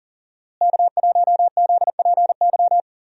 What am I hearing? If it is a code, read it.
Morse code, heard as a keyed tone, about 40 wpm, 700 Hz. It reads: K1ZPY
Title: USA Call Signs - 5 Characters - Repeated 3x - 40wpm